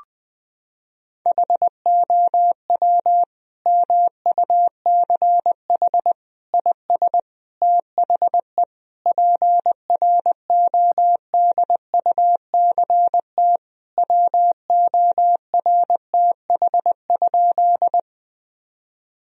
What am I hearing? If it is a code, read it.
Morse code, heard as a keyed tone, about 20 wpm, 700 Hz. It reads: HOW MUCH IS THE PRODUCT WORTH?
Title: Sentences from Top 1000 Words - Review Part 2 - 20wpm